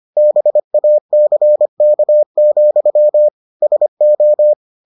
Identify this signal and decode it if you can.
Morse code, heard as a keyed tone, about 25 wpm, 600 Hz. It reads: BACK, SO